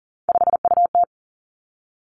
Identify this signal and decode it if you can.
Morse code, heard as a keyed tone, about 38 wpm, 700 Hz. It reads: 5UT
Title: Three Challenging Letter-Number Combinations 40wpm